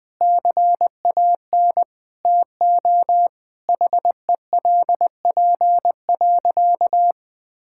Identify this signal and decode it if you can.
Morse code, heard as a keyed tone, about 20 wpm, 700 Hz. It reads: CAN TO HELP.
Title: Sentences with 8-Word Phrases - Spoken Beginning - 20wpm